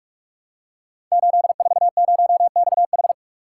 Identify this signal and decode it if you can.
Morse code, heard as a keyed tone, about 44 wpm, 700 Hz. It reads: 840XH
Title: Code Groups - 5 Letter:Numbers - 45wpm